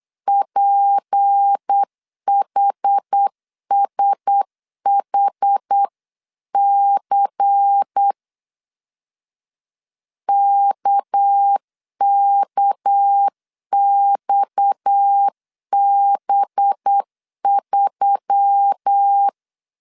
Morse code, heard as a keyed tone, about 8 wpm, 800 Hz. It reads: PHSHC KKXB3
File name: CW / DET / 1 9 1500C S B 8wpm 5624